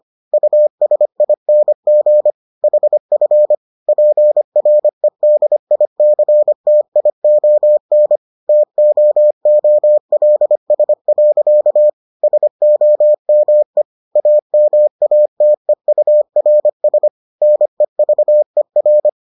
Morse code, heard as a keyed tone, about 25 wpm, 600 Hz. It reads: USING HF PREDICTION TOOLS. SOME AMATEURS NEVER